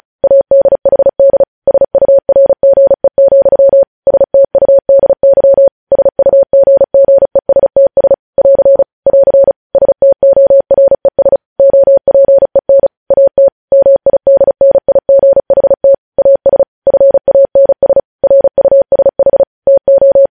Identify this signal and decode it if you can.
Morse code, heard as a keyed tone, about 36 wpm, 550 Hz. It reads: ADHD SURGE, STUDY SUGGESTS <AR> <AR> STORES OPEN AT MIDNIGHT AS FANS RUSH TO
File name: podcast / News headlines-1x-35x35